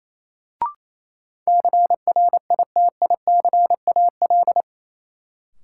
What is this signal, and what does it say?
Morse code, heard as a keyed tone, about 28 wpm, 700 Hz. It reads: CRITICAL
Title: Sentences from Top 1100 Words - New Word Drill - 28wpm